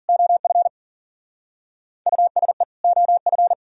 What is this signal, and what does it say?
Morse code, heard as a keyed tone, about 50 wpm, 700 Hz. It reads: OF USE OF
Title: Speed-Racing - Sets of 2 Words 40wpm